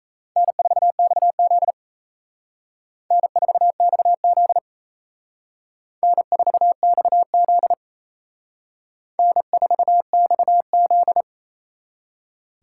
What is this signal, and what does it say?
Morse code, heard as a keyed tone, about 36 wpm, 700 Hz. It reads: N4XZ N4XZ N4XZ N4XZ